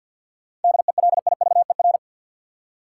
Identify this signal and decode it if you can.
Morse code, heard as a keyed tone, about 50 wpm, 700 Hz. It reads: DELIVER
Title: Seven Letter Words - 50wpm